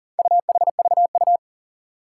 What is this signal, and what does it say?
Morse code, heard as a keyed tone, about 40 wpm, 700 Hz. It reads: UHVU
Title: Four Challenging Letter-Number Combinations 40wpm